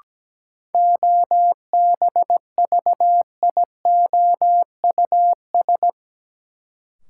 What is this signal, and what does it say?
Morse code, heard as a keyed tone, about 17 wpm, 700 Hz. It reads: OBVIOUS
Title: Sentences from Top 1600 Words - New Word Drill - 17wpm